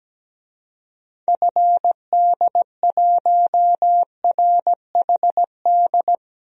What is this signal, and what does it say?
Morse code, heard as a keyed tone, about 17 wpm, 700 Hz. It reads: FD1RHD
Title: DXCC-100 Call Signs - 6 Characters - Repeated 2x - 17wpm